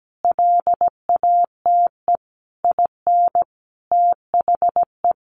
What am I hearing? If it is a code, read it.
Morse code, heard as a keyed tone, about 17 wpm, 700 Hz. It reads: LATE IN THE
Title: Sentences from Top 1400 Words 17wpm